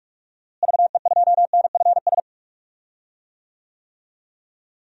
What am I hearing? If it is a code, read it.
Morse code, heard as a keyed tone, about 46 wpm, 700 Hz. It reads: VE2NFS
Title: DXCC-100 Call Signs - 6 Characters - Repeated 2x - 45wpm